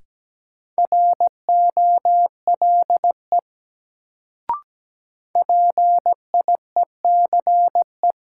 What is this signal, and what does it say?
Morse code, heard as a keyed tone, about 17 wpm, 700 Hz. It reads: ROLE PIECE
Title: Sets of 3 Words - New Word Drill - 17wpm